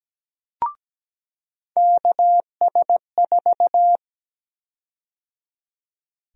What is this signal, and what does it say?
Morse code, heard as a keyed tone, about 17 wpm, 700 Hz. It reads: KS4
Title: USA Call Sign Prefixes - 17wpm